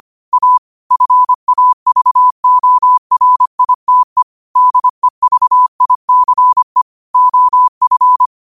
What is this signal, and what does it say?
Morse code, heard as a keyed tone, about 25 wpm, 1000 Hz. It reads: A FAVORITE DEVICE OF